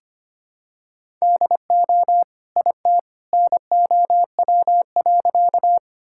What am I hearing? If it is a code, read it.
Morse code, heard as a keyed tone, about 25 wpm, 700 Hz. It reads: DO IT NOW.
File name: practice / 25wpm / Sentences with 2-Word Phrases - 25wpm